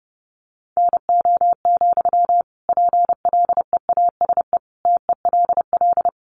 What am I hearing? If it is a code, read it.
Morse code, heard as a keyed tone, about 30 wpm, 700 Hz. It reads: NO, PLEASE TELL
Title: Sentences from Top 1500 Words 30wpm